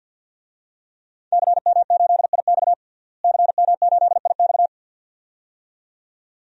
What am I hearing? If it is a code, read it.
Morse code, heard as a keyed tone, about 52 wpm, 700 Hz. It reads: CM8IX CM8IX
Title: DXCC-100 Call Signs - 5 Characters - Repeated 2x - 50wpm